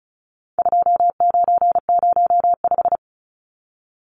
Morse code, heard as a keyed tone, about 34 wpm, 700 Hz. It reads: 2905
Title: Numbers - 4-Digit 35wpm